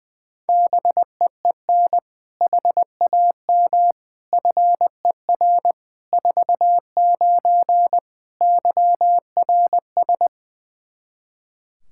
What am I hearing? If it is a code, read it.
Morse code, heard as a keyed tone, about 20 wpm, 700 Hz. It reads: BEEN HAM FER 49 YRS
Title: K1USN SST QSO Practice 20wpm